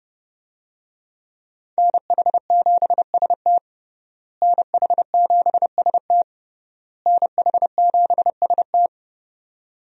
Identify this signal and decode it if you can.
Morse code, heard as a keyed tone, about 30 wpm, 700 Hz. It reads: NH7ST NH7ST NH7ST
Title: USA Call Signs - 5 Characters - Repeated 3x - 30wpm